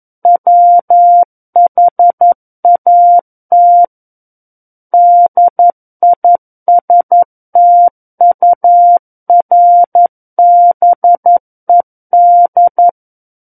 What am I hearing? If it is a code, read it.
Morse code, heard as a keyed tone, about 11 wpm, 700 Hz. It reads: WHAT DISTURBED
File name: Book One / 02-Chapter 2 - 11 WPM